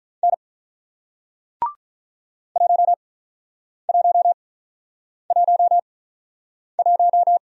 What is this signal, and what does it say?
Morse code, heard as a keyed tone, about 46 wpm, spaced as for 9 wpm, 700 Hz. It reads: N 1111